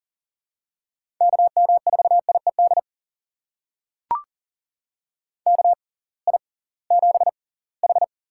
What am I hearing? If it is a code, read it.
Morse code, heard as a keyed tone, about 40 wpm, spaced as for 23 wpm, 700 Hz. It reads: KM4IED KI7H
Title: USA Call Signs - 6 Characters - 6x Character Spacing - 40wpm